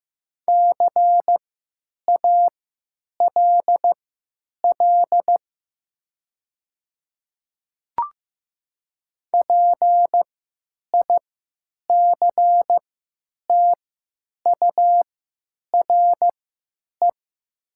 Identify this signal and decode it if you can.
Morse code, heard as a keyed tone, about 15 wpm, spaced as for 9 wpm, 700 Hz. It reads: CALL PICTURE
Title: 3x Character Spacing - Top 500 Words - 15wpm